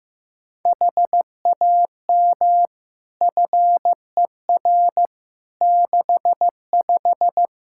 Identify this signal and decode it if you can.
Morse code, heard as a keyed tone, about 15 wpm, 700 Hz. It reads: HAM FER 65
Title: Practice QSO - Exchange 3 - More Info - 15wpm